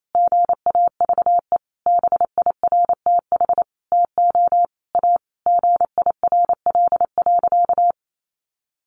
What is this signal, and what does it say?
Morse code, heard as a keyed tone, about 28 wpm, 700 Hz. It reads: GAVE BIRTH TO A GIRL.
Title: Sentences from Top 1100 Words 28wpm